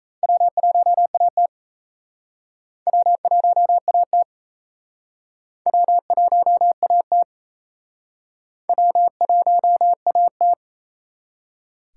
Morse code, heard as a keyed tone, about 34 wpm, 700 Hz. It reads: W1AT W1AT W1AT W1AT
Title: Speed Racing - USA 1x2 Call Signs 28wpm